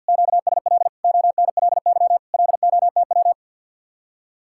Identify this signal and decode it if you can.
Morse code, heard as a keyed tone, about 60 wpm, 700 Hz. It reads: QSL ONLY LOTW
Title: Speed-Racing Common QSO Phrases - 50wpm